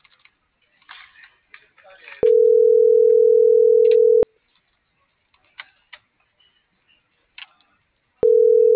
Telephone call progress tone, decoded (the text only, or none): ringback tone